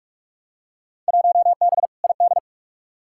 Morse code, heard as a keyed tone, about 46 wpm, 700 Hz. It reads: 1B ID
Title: Field Day Exchanges - 45wpm